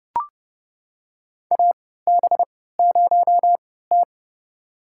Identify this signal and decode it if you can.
Morse code, heard as a keyed tone, about 30 wpm, spaced as for 17 wpm, 700 Hz. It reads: AB0T
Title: USA Call Signs - 4 Characters - 3x Character Spacing - 30wpm